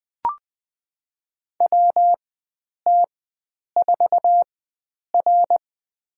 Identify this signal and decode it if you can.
Morse code, heard as a keyed tone, about 20 wpm, spaced as for 9 wpm, 700 Hz. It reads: WT4R
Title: USA Call Signs - 4 Characters - 4x Character Spacing - 20wpm